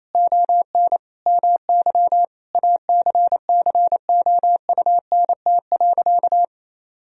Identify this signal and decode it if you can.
Morse code, heard as a keyed tone, about 28 wpm, 700 Hz. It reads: ON MY ACCOUNT.
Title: Sentences from Top 1000 Words - Review Part 3 - 28wpm